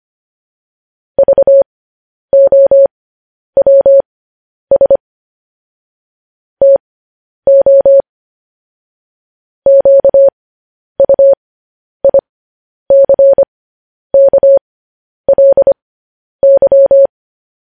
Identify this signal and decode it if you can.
Morse code, heard as a keyed tone, about 25 wpm, spaced as for 10 wpm, 550 Hz. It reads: VOWS TO QUICKLY